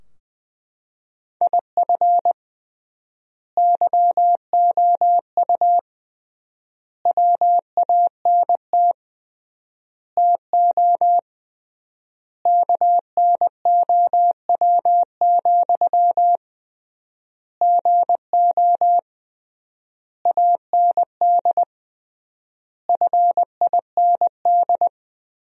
Morse code, heard as a keyed tone, about 20 wpm, 700 Hz. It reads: IF YOU WANT TO KNOW, GO AND FIND
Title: 3x Word-Spacing - Sentences from Top 200 Words - 20wpm